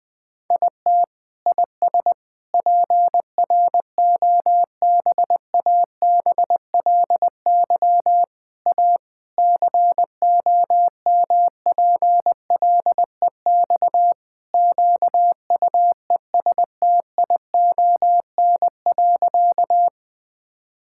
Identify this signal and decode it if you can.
Morse code, heard as a keyed tone, about 20 wpm, 700 Hz. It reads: IT IS PROBABLY A COMPLEX QUESTION.
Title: Sentences from Top 1000 Words - Review Part 6 - 20wpm